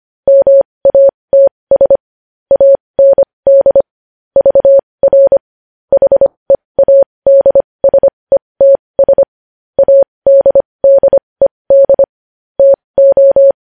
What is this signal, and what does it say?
Morse code, heard as a keyed tone, about 25 wpm, spaced as for 20 wpm, 550 Hz. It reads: MATS AND VR HEADSETS ADDED TO
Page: news-headlines-in-morse-code-at-20-wpm